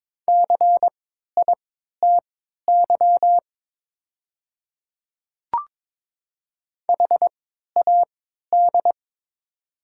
Morse code, heard as a keyed tone, about 22 wpm, spaced as for 12 wpm, 690 Hz. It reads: CITY HAD
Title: 3x Character Spacing - Top 300 Words - 22wpm